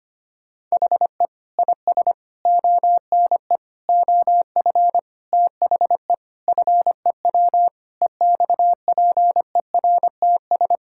Morse code, heard as a keyed tone, about 25 wpm, 700 Hz. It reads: HE IS ONE OF THE FEW EXPERTS
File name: practice / 25wpm / Sentences with 4-Word Phrases - 25wpm